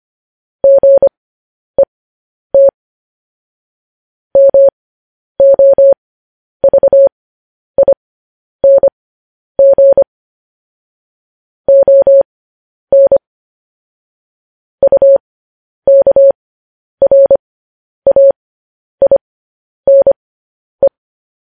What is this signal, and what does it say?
Morse code, heard as a keyed tone, about 25 wpm, spaced as for 10 wpm, 550 Hz. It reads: GET MOVING ON UKRAINE
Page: news headlines in morse code at 10 wpm